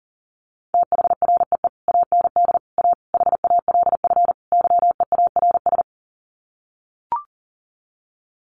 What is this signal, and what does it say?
Morse code, heard as a keyed tone, about 40 wpm, 700 Hz. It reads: THREE AND A HALF YEARS
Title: Sets of 5 Words Encore2 40wpm